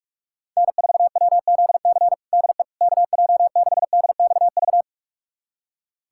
Morse code, heard as a keyed tone, about 46 wpm, 700 Hz. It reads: N4WZC DE KJ6DXV
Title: USA Callsign DE Callsign 45wpm